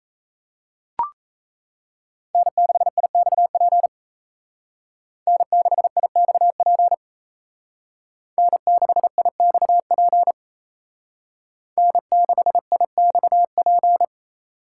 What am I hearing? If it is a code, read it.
Morse code, heard as a keyed tone, about 35 wpm, 700 Hz. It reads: N6IXP N6IXP N6IXP N6IXP